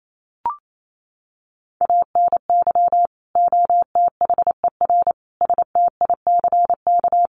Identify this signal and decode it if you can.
Morse code, heard as a keyed tone, about 28 wpm, 700 Hz. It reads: ANY OTHER STICK
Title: Sentences from Top 1200 Words 28wpm